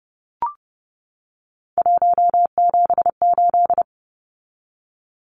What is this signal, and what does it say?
Morse code, heard as a keyed tone, about 30 wpm, 700 Hz. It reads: 178
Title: Numbers - 3-Digit 30wpm